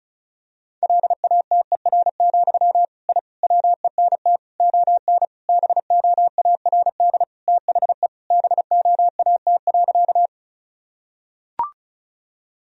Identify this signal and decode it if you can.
Morse code, heard as a keyed tone, about 36 wpm, 700 Hz. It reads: LATER, I WENT ON BOARD THE BOAT.
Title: Sentences from Top 1000 Words - Review Part 4 - 35wpm